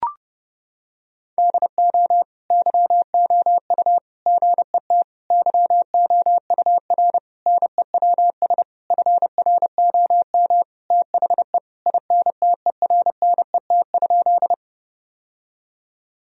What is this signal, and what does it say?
Morse code, heard as a keyed tone, about 32 wpm, 700 Hz. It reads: DO YOU GET YOUR NEWS FROM THE INTERNET?
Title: Sentences from Top 1000 Words - Review Part 4 - 30wpm